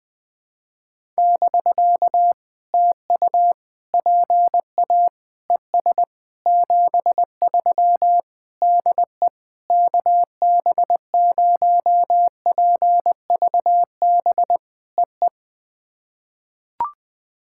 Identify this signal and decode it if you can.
Morse code, heard as a keyed tone, about 20 wpm, 700 Hz. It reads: <BK> TU PA ES 73 DE KB0PVB EE
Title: POTA QSO Practice - 20wpm